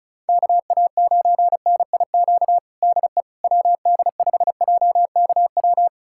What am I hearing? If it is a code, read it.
Morse code, heard as a keyed tone, about 36 wpm, 700 Hz. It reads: KA9NIQ DE WD5JKW
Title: USA Callsign DE Callsign - 6 Characters - 35wpm